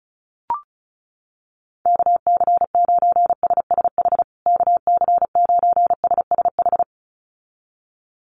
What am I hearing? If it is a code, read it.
Morse code, heard as a keyed tone, about 34 wpm, 700 Hz. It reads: KC9SSH KC9SSH
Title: USA Call Signs Repeated 2x 35wpm